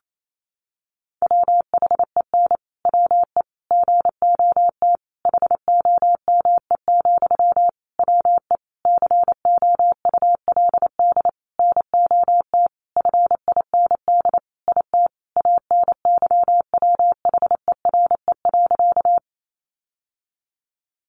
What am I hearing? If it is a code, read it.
Morse code, heard as a keyed tone, about 28 wpm, 700 Hz. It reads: WHEN WE GOT HOME, WE COULD NOT FIND IT ANYWHERE.
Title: Sentences from Top 1400 Words 28wpm